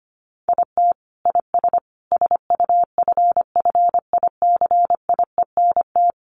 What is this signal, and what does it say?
Morse code, heard as a keyed tone, about 25 wpm, 700 Hz. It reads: IT IS SUFFICIENT
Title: Sentences from Top 1400 Words 25wpm